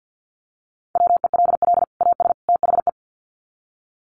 Morse code, heard as a keyed tone, about 50 wpm, 700 Hz. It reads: WELL AS THE
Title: Sets of 3 Words 50wpm